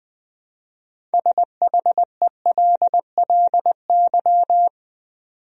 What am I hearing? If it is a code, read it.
Morse code, heard as a keyed tone, about 20 wpm, 700 Hz. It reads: SHELLY
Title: USA Common Names - 20wpm